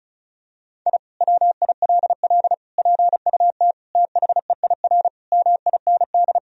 Morse code, heard as a keyed tone, about 36 wpm, 700 Hz. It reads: I WILL PUT THEIR MIND
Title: Sentences from Top 1000 Words - Review Part 1 - 35wpm